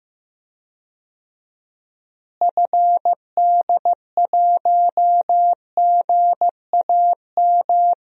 Morse code, heard as a keyed tone, about 16 wpm, 700 Hz. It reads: FD1GAM
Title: DXCC-100 Call Signs - 6 Characters - Repeated 2x - 15wpm